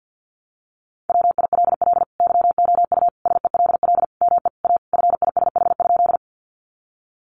Morse code, heard as a keyed tone, about 50 wpm, 700 Hz. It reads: WILL YOU SELL ME A FISH?
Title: Sentences from Top 1100 Words 50wpm